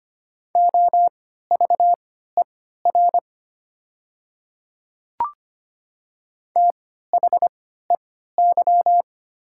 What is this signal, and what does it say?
Morse code, heard as a keyed tone, about 25 wpm, spaced as for 14 wpm, 700 Hz. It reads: OVER THEY
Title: 3x Character Spacing - Top 100 Words - 25wpm